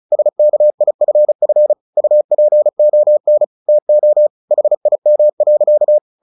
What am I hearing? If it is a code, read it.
Morse code, heard as a keyed tone, about 34 wpm, 600 Hz. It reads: SKIFF UPON TO HIM.
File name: the old man and the sea 2 35wpm